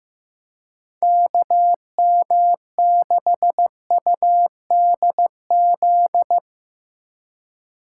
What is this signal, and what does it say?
Morse code, heard as a keyed tone, about 15 wpm, 700 Hz. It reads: KM6UDZ
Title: USA Call Signs - 6 Characters - 2x Character Spacing - 15wpm